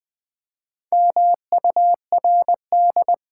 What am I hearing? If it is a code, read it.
Morse code, heard as a keyed tone, about 20 wpm, 700 Hz. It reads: MURD
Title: Build A Word - 2 Letters At A Time - 6-Letter Words - 20wpm